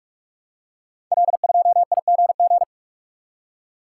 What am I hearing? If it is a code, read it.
Morse code, heard as a keyed tone, about 44 wpm, 690 Hz. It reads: L2IGG
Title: Code Groups - 5 Letter:Numbers - 45wpm